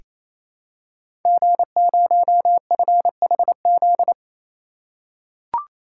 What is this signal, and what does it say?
Morse code, heard as a keyed tone, about 27 wpm, 700 Hz. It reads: G0FHZ
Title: Code Groups - 5 Letter:Numbers - 28wpm